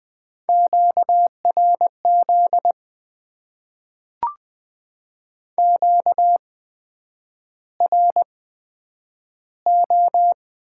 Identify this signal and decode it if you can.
Morse code, heard as a keyed tone, about 20 wpm, spaced as for 9 wpm, 700 Hz. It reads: QRZ QRO
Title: Top 90 QSO Elements - 8x Character Spacing - 20wpm